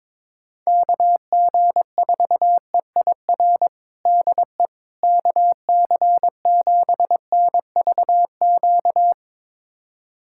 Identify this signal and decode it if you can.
Morse code, heard as a keyed tone, about 21 wpm, 700 Hz. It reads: KG4EIR DE KC7NVQ
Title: USA Callsign DE Callsign 22wpm